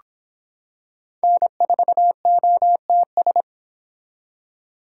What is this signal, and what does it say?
Morse code, heard as a keyed tone, about 26 wpm, 700 Hz. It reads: N4OTS